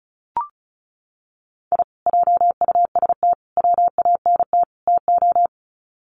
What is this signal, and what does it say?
Morse code, heard as a keyed tone, about 34 wpm, 700 Hz. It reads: I JUST WANT TO
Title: Sets of 4 Words 35wpm